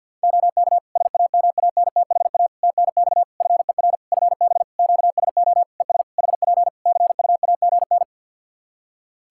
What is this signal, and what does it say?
Morse code, heard as a keyed tone, about 52 wpm, 700 Hz. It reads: OK SAMANTHA TNX FER FB QSO ES HP CUAGN
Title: Practice QSO - Exchange 4 - Ending - 50wpm